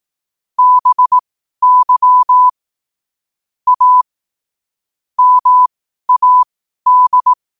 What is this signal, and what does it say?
Morse code, heard as a keyed tone, about 18 wpm, spaced as for 12 wpm, 1000 Hz. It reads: BY A MAD
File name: qotdfiles / 12wpm